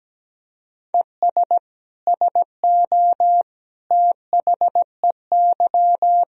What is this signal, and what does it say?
Morse code, heard as a keyed tone, about 17 wpm, 700 Hz. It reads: ES SO THEY
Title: Sentences from Top 400 Words and Abbreviations 17wpm